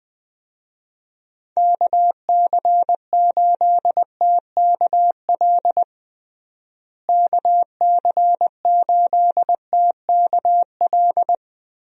Morse code, heard as a keyed tone, about 20 wpm, 700 Hz. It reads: KC8TKL KC8TKL
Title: USA Call Signs - 6 Characters - Repeated 3x - 20wpm